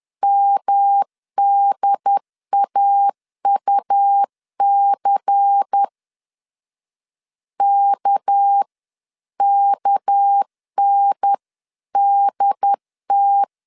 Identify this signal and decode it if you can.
Morse code, heard as a keyed tone, about 10 wpm, 800 Hz. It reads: MDAUC KKNDT
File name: CW / DET / 1 6 1500C S B 10wpm 4137